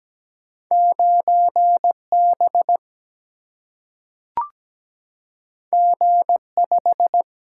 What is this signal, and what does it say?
Morse code, heard as a keyed tone, about 17 wpm, 700 Hz. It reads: 9 B G 5
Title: Single-Letter Single-Number 17wpm